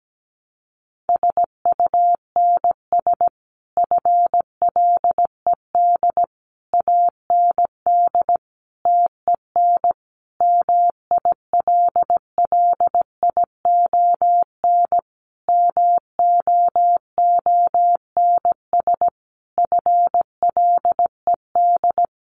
Morse code, heard as a keyed tone, about 17 wpm, 700 Hz. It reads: SUNS FLED AND TEN MILLION MOONS FLED
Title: A Sound Of Thunder 17wpm